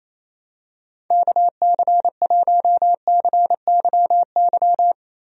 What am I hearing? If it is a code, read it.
Morse code, heard as a keyed tone, about 28 wpm, 700 Hz. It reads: KC1CYY
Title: USA Call Signs - 6 Characters - Repeated 2x - 28wpm